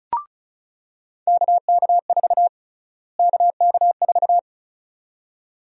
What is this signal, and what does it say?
Morse code, heard as a keyed tone, about 36 wpm, 700 Hz. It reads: KK4 KK4
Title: USA Call Sign Prefixes - Repeated 2x - 35wpm